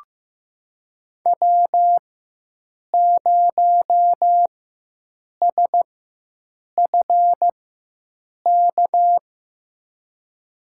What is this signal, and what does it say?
Morse code, heard as a keyed tone, about 15 wpm, spaced as for 7 wpm, 700 Hz. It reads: W0SFK
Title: USA Call Signs - 4x Character Spacing - 15wpm